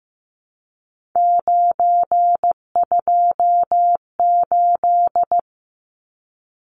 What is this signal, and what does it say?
Morse code, heard as a keyed tone, about 15 wpm, 700 Hz. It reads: 928
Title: Numbers - 3-Digit 15wpm